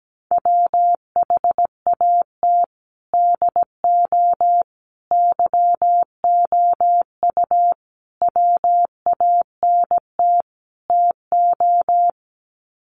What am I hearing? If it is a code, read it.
Morse code, heard as a keyed tone, about 17 wpm, 700 Hz. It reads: WHAT DO YOU WANT TO
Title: Sets of 5 Words 17wpm